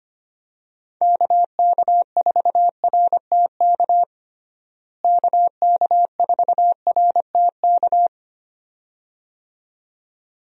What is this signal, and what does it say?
Morse code, heard as a keyed tone, about 25 wpm, 700 Hz. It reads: KK4RTK KK4RTK
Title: USA Call Signs - 6 Characters - Repeated 2x - 25wpm